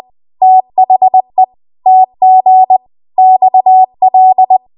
Morse code, heard as a keyed tone, about 20 wpm, 750 Hz. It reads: THE TG XL
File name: Archive / 20 WPM / 20WPM